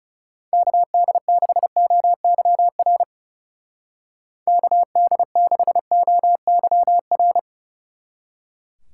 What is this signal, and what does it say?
Morse code, heard as a keyed tone, about 34 wpm, 700 Hz. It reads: KD6OYR KD6OYR
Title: Speed Racing - USA 2x3 Call Signs 30wpm